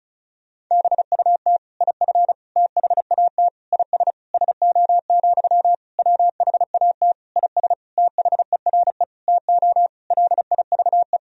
Morse code, heard as a keyed tone, about 36 wpm, 700 Hz. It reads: BUT IF THAT IS SO, WHAT IS THERE TO LIVE
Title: Sentences from Top 200 Words and Abbreviations 35wpm